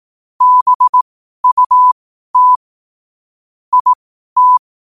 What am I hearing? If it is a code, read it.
Morse code, heard as a keyed tone, about 18 wpm, spaced as for 12 wpm, 1000 Hz. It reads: BUT IT
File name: qotdfiles / 12wpm